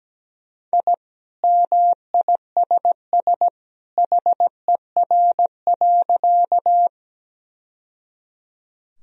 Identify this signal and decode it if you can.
Morse code, heard as a keyed tone, about 17 wpm, 700 Hz. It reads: I MISS HER.
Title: Sentences from Top 1000 Words - Review Part 3 - 17wpm